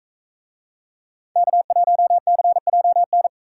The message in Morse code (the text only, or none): K1CJN